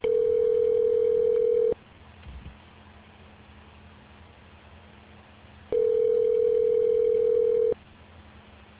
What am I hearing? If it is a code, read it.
Ringback tone (North America), 440 + 480 Hz.